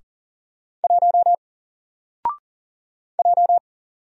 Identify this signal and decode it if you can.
Morse code, heard as a keyed tone, about 38 wpm, 700 Hz. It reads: 1 J
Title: Single Letter-Number Alt 40wpm